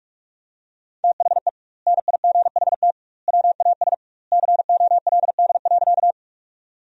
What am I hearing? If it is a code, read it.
Morse code, heard as a keyed tone, about 46 wpm, 700 Hz. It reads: THE NIGHT WAS COLD.
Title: 2x Word-Spacing - Sentences from Top 400 Words - 45wpm